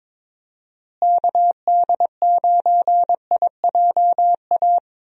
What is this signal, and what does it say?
Morse code, heard as a keyed tone, about 22 wpm, 700 Hz. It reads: KD9IJA
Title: USA Call Signs - 6 Characters - Repeated 3x - 22wpm